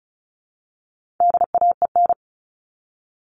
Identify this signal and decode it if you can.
Morse code, heard as a keyed tone, about 34 wpm, 700 Hz. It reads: DAEN
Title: All Cut Numbers - 4-Digit 35wpm